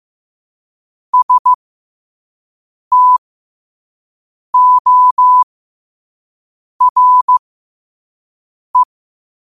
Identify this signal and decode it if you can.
Morse code, heard as a keyed tone, about 15 wpm, spaced as for 5 wpm, 1000 Hz. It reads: STORE
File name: qotdfiles / 5wpm